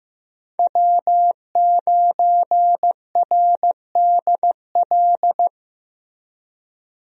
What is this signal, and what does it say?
Morse code, heard as a keyed tone, about 15 wpm, 700 Hz. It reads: W9RDL
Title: USA Call Signs - 5 Characters - Repeated 2x - 15wpm